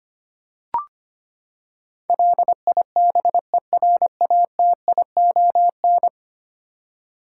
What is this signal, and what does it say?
Morse code, heard as a keyed tone, about 25 wpm, 700 Hz. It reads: LIBERATION
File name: Ten Letter Words - 25wpm